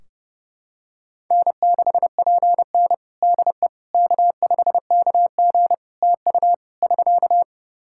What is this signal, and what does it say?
Morse code, heard as a keyed tone, about 30 wpm, 700 Hz. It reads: N6PN DE K5KG TU <SK>
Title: Practice QSO - Exchange 4 - Ending - 30wpm